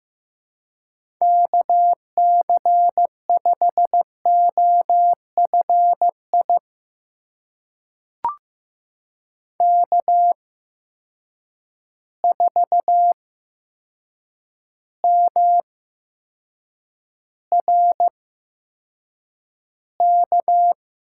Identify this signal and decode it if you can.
Morse code, heard as a keyed tone, about 15 wpm, spaced as for 7 wpm, 700 Hz. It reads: KC5OFI K4MRK
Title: USA Call Signs - 8x Character Spacing - 15wpm